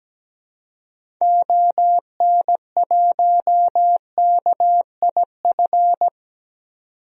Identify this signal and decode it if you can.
Morse code, heard as a keyed tone, about 17 wpm, 700 Hz. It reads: ON1KIF
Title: DXCC-100 Call Signs - 6 Characters - 17wpm